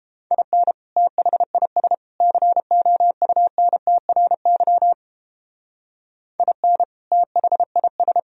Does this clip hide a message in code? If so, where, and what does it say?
Yes: Morse code from 0.3 to 8.2 s, heard as a keyed tone, about 31 wpm, 700 Hz. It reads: IN THIS COUNTRY IN THIS